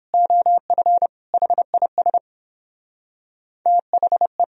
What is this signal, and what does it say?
Morse code, heard as a keyed tone, about 31 wpm, 700 Hz. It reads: OF HIS THE